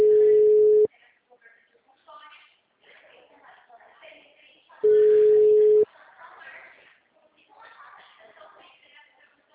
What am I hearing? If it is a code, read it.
Ringback tone (Europe, 425 Hz), with speech.